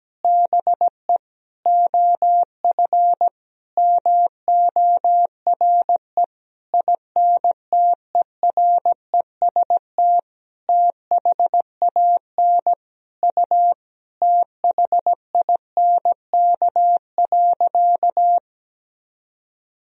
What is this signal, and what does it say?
Morse code, heard as a keyed tone, about 17 wpm, 700 Hz. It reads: BE OF MORE INTEREST THAN U THINK.
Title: Sentences from Top 400 Words and Abbreviations 17wpm